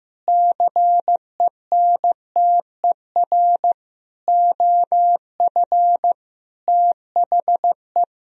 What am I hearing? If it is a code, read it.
Morse code, heard as a keyed tone, about 16 wpm, 700 Hz. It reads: CENTER OF THE
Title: Sentences from Top 1000 Words - Review Part 2 - 15wpm